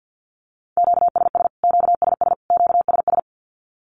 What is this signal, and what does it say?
Morse code, heard as a keyed tone, about 50 wpm, 700 Hz. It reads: QSS QSS QSS
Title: Q-Codes Repeated 3x 50wpm